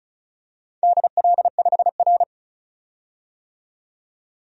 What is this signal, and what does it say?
Morse code, heard as a keyed tone, about 36 wpm, 700 Hz. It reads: DL5R
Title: DXCC-100 Call Signs - 4 Characters - 35wpm